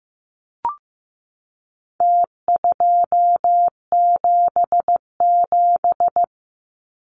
Morse code, heard as a keyed tone, about 15 wpm, 700 Hz. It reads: T277
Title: Numbers - Contest Style - 4-Digit 15wpm